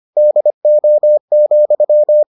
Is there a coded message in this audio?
Yes: Morse code from 0.2 to 2.2 s, heard as a keyed tone, about 25 wpm, 600 Hz. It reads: DO,